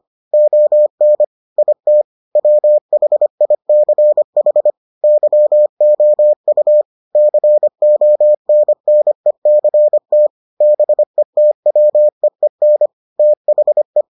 Morse code, heard as a keyed tone, about 25 wpm, 600 Hz. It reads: ON IT WHICH YOU CONNECT BETWEEN THE